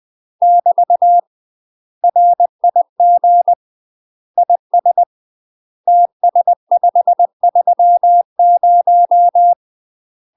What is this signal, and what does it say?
Morse code, heard as a keyed tone, about 20 wpm, 700 Hz. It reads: <BT> RIG IS TS530
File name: UK MASTER QSOs TRAINING / 20WPM / UK MASTER QSOs 0008